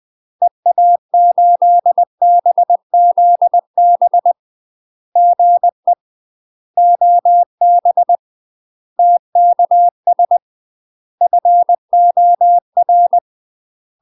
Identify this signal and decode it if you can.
Morse code, heard as a keyed tone, about 20 wpm, 700 Hz. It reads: EA8BZB GE OB TKS FOR
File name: UK MASTER QSOs TRAINING / 20WPM / UK MASTER QSOs 0034